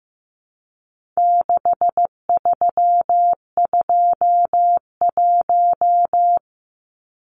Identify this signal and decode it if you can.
Morse code, heard as a keyed tone, about 15 wpm, 700 Hz. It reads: 6321
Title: Numbers - 4-Digit 15wpm